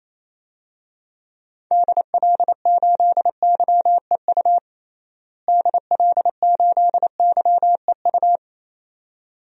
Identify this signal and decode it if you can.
Morse code, heard as a keyed tone, about 28 wpm, 700 Hz. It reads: DL8YEU DL8YEU
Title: DXCC-100 Call Signs - 6 Characters - Repeated 2x - 28wpm